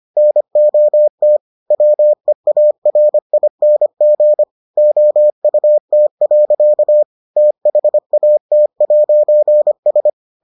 Morse code, heard as a keyed tone, about 25 wpm, 600 Hz. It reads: NOT WEARING OUT. THAT'S